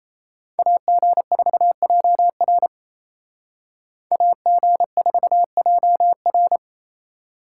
Morse code, heard as a keyed tone, about 36 wpm, 700 Hz. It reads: AG4JR AG4JR